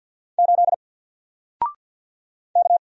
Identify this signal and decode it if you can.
Morse code, heard as a keyed tone, about 48 wpm, 700 Hz. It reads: 8 K
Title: Single Letter-Number Alt 50wpm